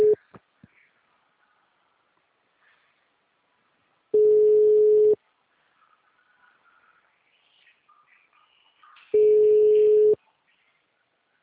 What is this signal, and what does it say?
Ringback tone (Europe, 425 Hz).